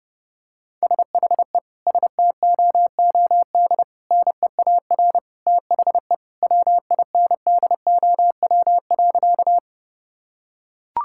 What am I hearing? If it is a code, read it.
Morse code, heard as a keyed tone, about 30 wpm, 700 Hz. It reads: SHE STOOD NEAR THE WINDOW.
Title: Sentences from Top 1600 Words 30wpm